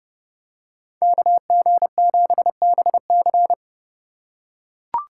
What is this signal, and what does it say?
Morse code, heard as a keyed tone, about 29 wpm, 700 Hz. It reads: KG7BC